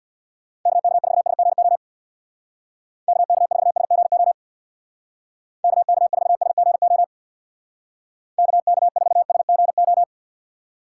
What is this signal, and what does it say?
Morse code, heard as a keyed tone, about 65 wpm, 700 Hz. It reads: KK4SGO KK4SGO KK4SGO KK4SGO